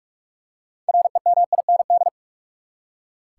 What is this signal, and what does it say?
Morse code, heard as a keyed tone, about 46 wpm, 700 Hz. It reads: REMIND
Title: Six Letter Words - 45wpm